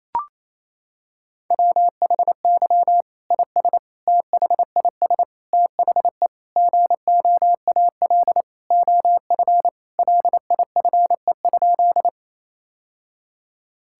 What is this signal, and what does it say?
Morse code, heard as a keyed tone, about 28 wpm, 700 Hz. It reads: WHY IS THIS THE GOAL OF LIFE?
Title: Sentences from Top 1000 Words - Review Part 5 - 28wpm